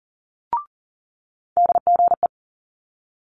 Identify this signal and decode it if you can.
Morse code, heard as a keyed tone, about 41 wpm, 700 Hz. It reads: DGE
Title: Multi-letter Phonemes Quick Fire 40wpm